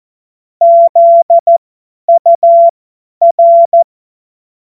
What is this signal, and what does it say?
Morse code, heard as a keyed tone, about 14 wpm, spaced as for 10 wpm, 680 Hz. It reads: ZUR
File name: Morsekurs / Witze / Witz 07. März 2023 50 F70